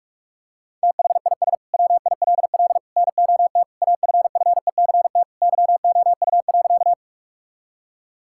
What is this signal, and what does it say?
Morse code, heard as a keyed tone, about 41 wpm, 700 Hz. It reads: THIS WILL NOT AFFECT YOU.
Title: Sentences from Top 1000 Words - Review Part 6 - 45wpm